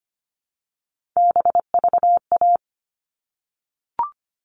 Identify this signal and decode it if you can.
Morse code, heard as a keyed tone, about 25 wpm, 700 Hz. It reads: BVA
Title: All Cut Numbers - 3-Digit 25wpm